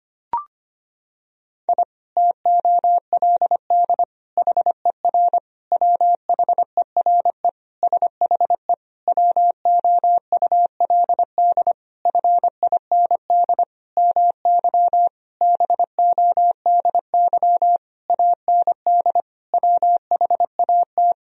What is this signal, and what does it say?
Morse code, heard as a keyed tone, about 25 wpm, 700 Hz. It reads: I TOLD HER WHERE SHE WOULD FIND MY BODY AND WHAT
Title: Sentences from Top 1000 Words - Review Part 2 - 25wpm